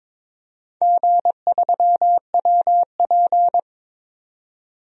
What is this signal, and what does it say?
Morse code, heard as a keyed tone, about 22 wpm, 700 Hz. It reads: G3WP
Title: DXCC-100 Call Signs - 4 Characters - 22wpm